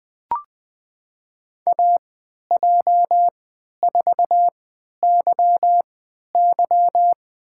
Morse code, heard as a keyed tone, about 20 wpm, spaced as for 11 wpm, 700 Hz. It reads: AJ4YY
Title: USA Call Signs - 5 Characters - 3x Character Spacing - 20wpm